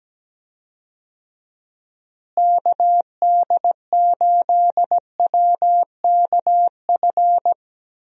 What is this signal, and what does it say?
Morse code, heard as a keyed tone, about 17 wpm, 700 Hz. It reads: KD8WKF
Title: USA Call Signs - 6 Characters - Repeated 3x - 17wpm